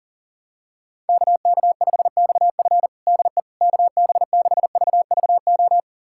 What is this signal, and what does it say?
Morse code, heard as a keyed tone, about 40 wpm, 700 Hz. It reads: KK5XF DE KB6VVO
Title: USA Callsign DE Callsign 40wpm